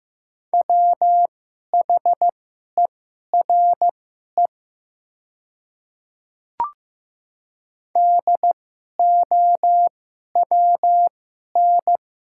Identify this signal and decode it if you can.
Morse code, heard as a keyed tone, about 15 wpm, spaced as for 11 wpm, 690 Hz. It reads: WHERE DOWN
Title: 2x Character Spacing - Top 200 Words - 15wpm